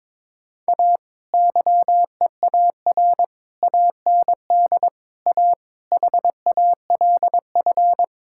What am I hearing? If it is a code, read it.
Morse code, heard as a keyed tone, about 22 wpm, 700 Hz. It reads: A YEAR AND A HALF
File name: practice / 22wpm / Build A Phrase - 5-Word Phrases - 22wpm